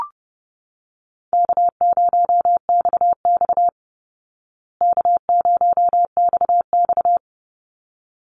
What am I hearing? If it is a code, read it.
Morse code, heard as a keyed tone, about 30 wpm, 700 Hz. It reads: K0XX K0XX
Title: Easy Call Signs Repeated 2x - 30wpm